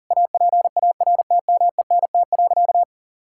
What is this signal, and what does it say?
Morse code, heard as a keyed tone, about 40 wpm, 700 Hz. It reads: APARTMENT.
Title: Sentences from Top 1000 Words - Review Part 6 - 40wpm